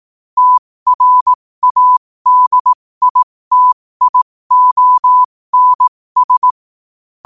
Morse code, heard as a keyed tone, about 18 wpm, spaced as for 15 wpm, 1000 Hz. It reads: TRADITIONS
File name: qotdfiles / 15wpm